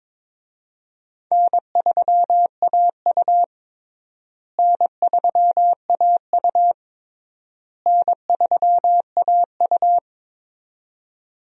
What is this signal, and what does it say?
Morse code, heard as a keyed tone, about 22 wpm, 700 Hz. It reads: N3AU N3AU N3AU
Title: USA Call Signs - 4 Characters - Repeated 3x - 22wpm